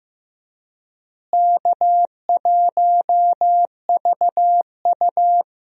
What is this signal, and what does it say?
Morse code, heard as a keyed tone, about 15 wpm, 700 Hz. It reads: K1VU
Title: USA Call Signs - 4 Characters - Repeated 3x - 15wpm